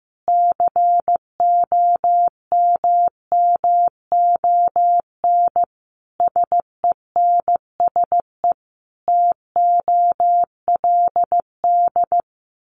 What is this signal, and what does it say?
Morse code, heard as a keyed tone, about 15 wpm, 700 Hz. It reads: COMMON SENSE TOLD
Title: Sentences from Top 1200 Words 15wpm